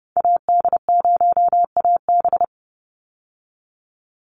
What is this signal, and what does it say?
Morse code, heard as a keyed tone, about 30 wpm, 700 Hz. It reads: AD0AB
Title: Call Signs Repeated 3x - 30wpm